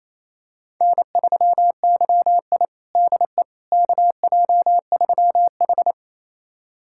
Morse code, heard as a keyed tone, about 28 wpm, 700 Hz. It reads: N3YI DE KJ3H
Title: USA Callsign DE Callsign - 4 Characters - 28wpm